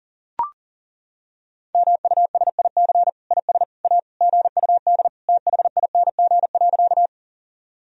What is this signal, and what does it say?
Morse code, heard as a keyed tone, about 40 wpm, 700 Hz. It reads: MUSIC IS A GUD THING.
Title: Sentences from Top 300 Words and Abbreviations 40wpm